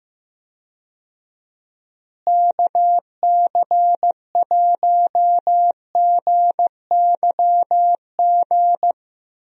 Morse code, heard as a keyed tone, about 15 wpm, 700 Hz. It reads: KC1GYG
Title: USA Call Signs - 6 Characters - Repeated 2x - 15wpm